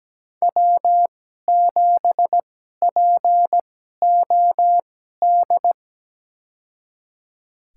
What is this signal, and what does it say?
Morse code, heard as a keyed tone, about 17 wpm, spaced as for 12 wpm, 700 Hz. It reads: W7POD
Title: USA Call Signs - 5 Characters - 2x Character Spacing - 17wpm